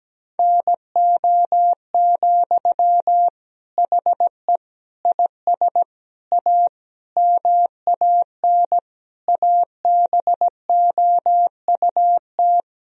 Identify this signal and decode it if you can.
Morse code, heard as a keyed tone, about 17 wpm, 700 Hz. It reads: NO, HE IS A MAN ABOUT
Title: Sentences from Top 1000 Words - Review Part 2 - 17wpm